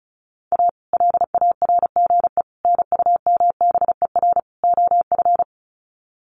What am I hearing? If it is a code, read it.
Morse code, heard as a keyed tone, about 34 wpm, 700 Hz. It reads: A LARGE NUMBER OF
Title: Sets of 4 Words 35wpm